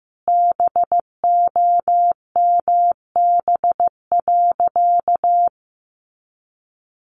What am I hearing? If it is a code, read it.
Morse code, heard as a keyed tone, about 15 wpm, 700 Hz. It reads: BOMB.
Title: A Sound Of Thunder 15wpm